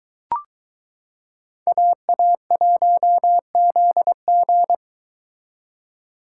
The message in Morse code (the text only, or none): AA1ZG